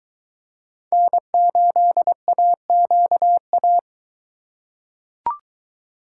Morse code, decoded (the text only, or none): N8AQA